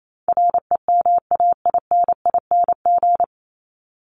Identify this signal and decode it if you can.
Morse code, heard as a keyed tone, about 28 wpm, 700 Hz. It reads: REMAINING
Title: ING Suffix - Top 100 Words with Base Words 28wpm